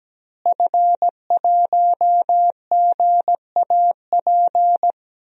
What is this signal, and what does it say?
Morse code, heard as a keyed tone, about 17 wpm, 700 Hz. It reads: F1GAP
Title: DXCC-100 Call Signs - 5 Characters - Repeated 3x - 17wpm